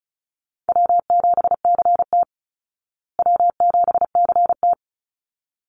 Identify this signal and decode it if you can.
Morse code, heard as a keyed tone, about 34 wpm, 700 Hz. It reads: W7CT W7CT
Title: Easy Call Signs Repeated 2x - 35wpm